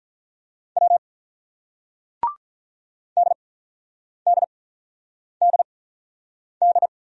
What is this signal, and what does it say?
Morse code, heard as a keyed tone, about 46 wpm, spaced as for 9 wpm, 700 Hz. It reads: W DDDD